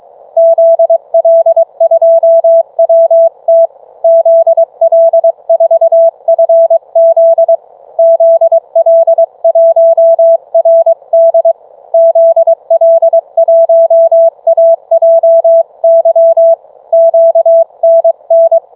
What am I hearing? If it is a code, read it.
Morse code, heard as a keyed tone, about 22 wpm, 660 Hz. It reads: ZL2WT ZL4FZ ZL1RD ZL1AJY QNN